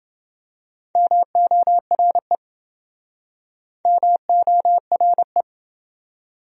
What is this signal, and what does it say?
Morse code, heard as a keyed tone, about 27 wpm, 700 Hz. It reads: MORE MORE